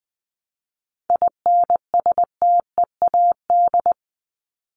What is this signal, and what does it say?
Morse code, heard as a keyed tone, about 20 wpm, 700 Hz. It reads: INSTEAD
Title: Top 501-750 Words 20wpm